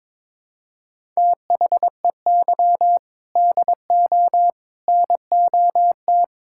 Morse code, heard as a keyed tone, about 21 wpm, 700 Hz. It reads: THEY DO NOT
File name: practice / 22wpm / Build A Phrase - 5-Word Phrases - 22wpm